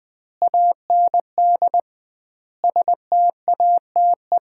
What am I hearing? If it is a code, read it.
Morse code, heard as a keyed tone, about 21 wpm, 700 Hz. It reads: AND STATE
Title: 2x Word-Spacing - Sets of 5 Words - 20wpm